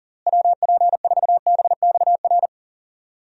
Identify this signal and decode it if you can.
Morse code, heard as a keyed tone, about 37 wpm, 700 Hz. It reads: WP4BXR
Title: USA Call Signs - 6 Characters - Repeated 2x - 40wpm